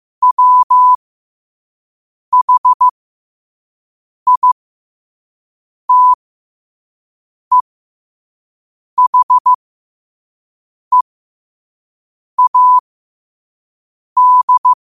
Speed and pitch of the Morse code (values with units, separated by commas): 15 wpm, spaced as for 5 wpm, 1000 Hz